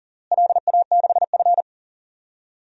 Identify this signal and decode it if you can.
Morse code, heard as a keyed tone, about 40 wpm, 700 Hz. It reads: LA6F
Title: DXCC-100 Call Signs - 4 Characters - Repeated 3x - 40wpm